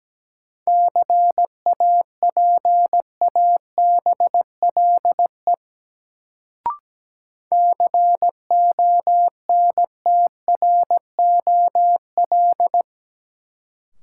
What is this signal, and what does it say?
Morse code, heard as a keyed tone, about 17 wpm, 700 Hz. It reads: CAPABLE CONTROL
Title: Sentences from Top 1200 Words - New Word Drill Alt - 17wpm